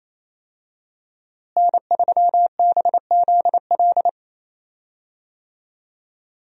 Morse code, heard as a keyed tone, about 28 wpm, 700 Hz. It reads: N3BZL
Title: USA Call Signs - 5 Characters - Repeated 3x - 28wpm